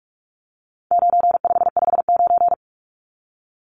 Morse code, heard as a keyed tone, about 44 wpm, 700 Hz. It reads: 9559
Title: Numbers - 4-Digit 45wpm